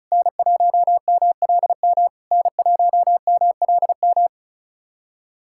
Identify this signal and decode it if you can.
Morse code, heard as a keyed tone, about 36 wpm, 690 Hz. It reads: N1MLM N1MLM
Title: SOTA QSO Practice 35wpm